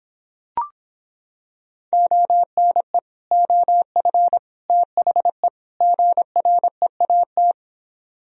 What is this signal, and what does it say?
Morse code, heard as a keyed tone, about 26 wpm, 700 Hz. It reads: ONE OF THE GREAT